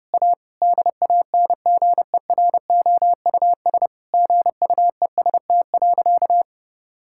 Morse code, heard as a keyed tone, about 30 wpm, 700 Hz. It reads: A DANGEROUS GUEST.
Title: Sentences from Top 1600 Words 30wpm